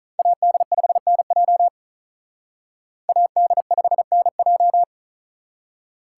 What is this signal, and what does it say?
Morse code, heard as a keyed tone, about 43 wpm, 700 Hz. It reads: AD5NJ AD5NJ
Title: Speed Racing - USA 2x2 Call Signs 35wpm